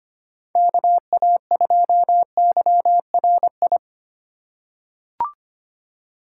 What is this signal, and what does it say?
Morse code, heard as a keyed tone, about 25 wpm, 700 Hz. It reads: KA2YRI
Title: USA Call Signs - 2x Character Spacing - 25wpm